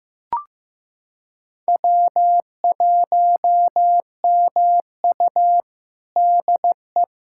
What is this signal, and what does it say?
Morse code, heard as a keyed tone, about 15 wpm, 700 Hz. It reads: W1MU DE
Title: USA Callsign DE Callsign - 4 Characters - 15wpm